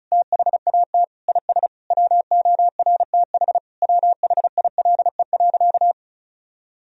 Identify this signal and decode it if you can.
Morse code, heard as a keyed tone, about 34 wpm, 700 Hz. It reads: THAT IS WORTH WHILE.
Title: Sentences from Top 1000 Words - Review Part 1 - 35wpm